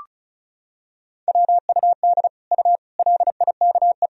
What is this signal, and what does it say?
Morse code, heard as a keyed tone, about 36 wpm, 700 Hz. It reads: WUD U LIKE
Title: Sentences from Top 100 Words and Abbreviations 35wpm